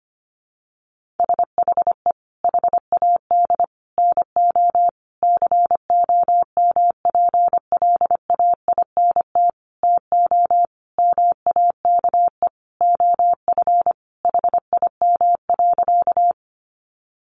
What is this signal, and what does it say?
Morse code, heard as a keyed tone, about 25 wpm, 700 Hz. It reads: SHE HAD NO COMPLAINT TO MAKE OF HIM.
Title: Sentences from Top 1400 Words 25wpm